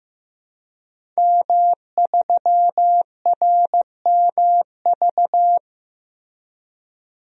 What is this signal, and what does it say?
Morse code, heard as a keyed tone, about 15 wpm, 700 Hz. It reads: M3RMV
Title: DXCC-100 Call Signs - 5 Characters - Repeated 2x - 15wpm